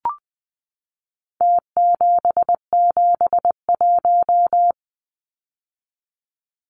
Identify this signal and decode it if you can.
Morse code, heard as a keyed tone, about 20 wpm, 700 Hz. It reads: T771
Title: Numbers - Contest Style - 4-Digit 20wpm